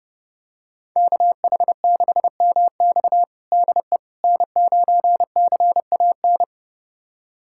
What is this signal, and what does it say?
Morse code, heard as a keyed tone, about 30 wpm, 700 Hz. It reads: KH6MX DE N9CAN
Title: USA Callsign DE Callsign - 5 Characters - 30wpm